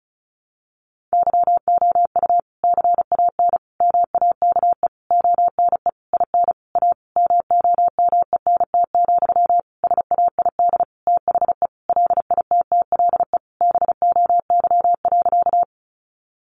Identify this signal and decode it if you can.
Morse code, heard as a keyed tone, about 34 wpm, 700 Hz. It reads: YOU CAN MAKE ONE IN A MOMENT, SAID THE LITTLE BOY.
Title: Sentences from Top 1000 Words 35wpm